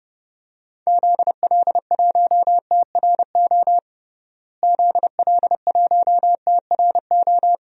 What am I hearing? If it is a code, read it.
Morse code, heard as a keyed tone, about 32 wpm, 700 Hz. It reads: ZL1TRO ZL1TRO
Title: DXCC-100 Call Signs - 6 Characters - Repeated 2x - 30wpm